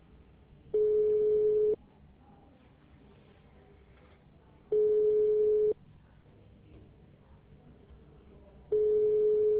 Ringback tone (425 Hz, 1 s on, 3 s off).